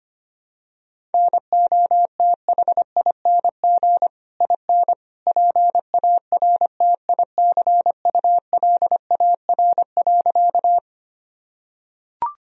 Morse code, heard as a keyed tone, about 25 wpm, 700 Hz. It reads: NOTHING IN PARTICULAR.
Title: Sentences from Top 1000 Words - Review Part 4 - 25wpm